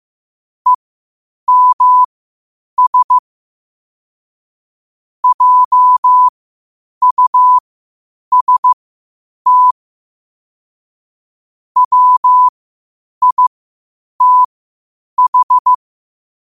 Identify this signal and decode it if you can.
Morse code, heard as a keyed tone, about 15 wpm, spaced as for 8 wpm, 1000 Hz. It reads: EMS JUST WITH
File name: qotdfiles / 8wpm